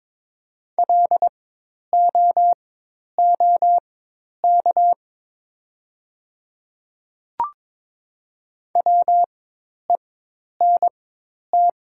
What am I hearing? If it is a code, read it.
Morse code, heard as a keyed tone, about 22 wpm, spaced as for 10 wpm, 700 Hz. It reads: LOOK WENT
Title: 4x Character Spacing - Top 200 Words - 22wpm